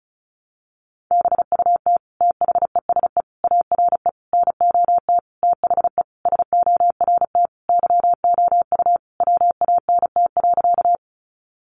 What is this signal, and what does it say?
Morse code, heard as a keyed tone, about 34 wpm, 700 Hz. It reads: BUT THESE ARE NOT THE SORT YOU WANT.
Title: Sentences from Top 1300 Words 35wpm